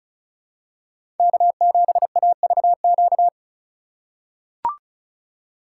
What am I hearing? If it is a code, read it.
Morse code, heard as a keyed tone, about 34 wpm, 700 Hz. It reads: K7AVQ
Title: USA Call Signs - 5 Characters - 4x Character Spacing - 35wpm